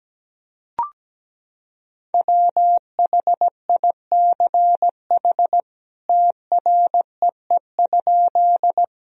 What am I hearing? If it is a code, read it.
Morse code, heard as a keyed tone, about 17 wpm, 700 Hz. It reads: WHICH TREE?
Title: Sentences from Top 1000 Words - Review Part 1 - 17wpm